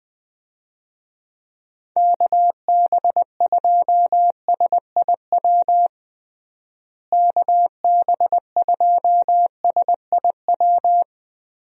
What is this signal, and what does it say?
Morse code, heard as a keyed tone, about 20 wpm, 700 Hz. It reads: KB2SIW KB2SIW
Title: USA Call Signs - 6 Characters - Repeated 3x - 20wpm